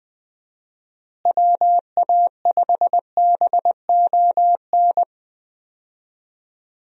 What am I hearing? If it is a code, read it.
Morse code, heard as a keyed tone, about 20 wpm, 700 Hz. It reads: WA5BON
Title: USA Call Signs - 6 Characters - Repeated 2x - 20wpm